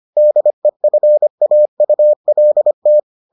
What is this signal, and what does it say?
Morse code, heard as a keyed tone, about 25 wpm, 600 Hz. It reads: DEFAULT